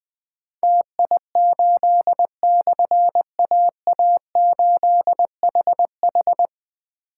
Morse code, heard as a keyed tone, about 21 wpm, 700 Hz. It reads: TI8/AA8HH
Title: Difficult Call Signs 20wpm